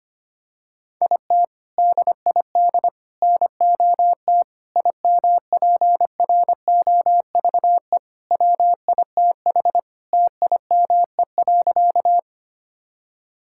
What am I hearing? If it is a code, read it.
Morse code, heard as a keyed tone, about 25 wpm, 700 Hz. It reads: IT DID NOT IMPROVE WITH TIME.
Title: Sentences from Top 1000 Words - Review Part 5 - 25wpm